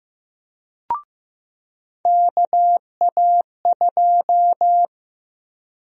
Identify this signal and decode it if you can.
Morse code, heard as a keyed tone, about 16 wpm, 700 Hz. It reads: KA2
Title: USA Call Sign Prefixes - 15wpm